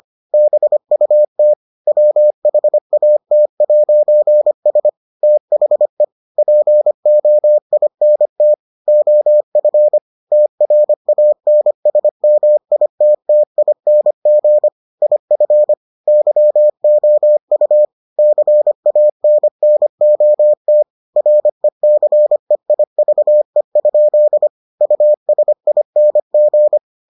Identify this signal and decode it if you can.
Morse code, heard as a keyed tone, about 25 wpm, 600 Hz. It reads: BUT WHAT'S THE POINT OF TRANSMITTING IF YOU CANNOT RECEIVE? USING